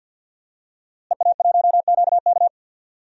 Morse code, heard as a keyed tone, about 46 wpm, 700 Hz. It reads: EA1QK
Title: DXCC-100 Call Signs - 5 Characters - Repeated 3x - 50wpm